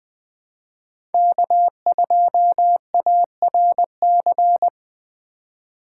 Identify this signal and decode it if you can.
Morse code, heard as a keyed tone, about 20 wpm, 700 Hz. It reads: K2ARC
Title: Speed Racing - USA Call Signs 17wpm